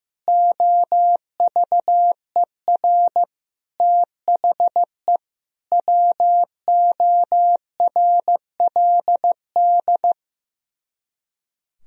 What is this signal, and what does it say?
Morse code, heard as a keyed tone, about 15 wpm, 700 Hz. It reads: OVER THE WORLD
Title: Speed-Racing - Sets of 5 Words 15wpm